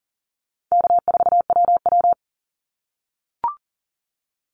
Morse code, heard as a keyed tone, about 39 wpm, 700 Hz. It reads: K4WW
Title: Easy Call Signs - 40wpm